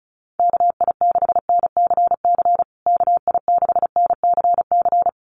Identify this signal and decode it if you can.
Morse code, heard as a keyed tone, about 34 wpm, 700 Hz. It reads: KI6NCC KI6NCC
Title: USA Call Signs Repeated 3x 35wpm